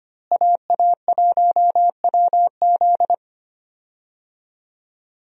Morse code, heard as a keyed tone, about 25 wpm, 700 Hz. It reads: AA1WZ
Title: USA Call Signs - 5 Characters - Repeated 2x - 25wpm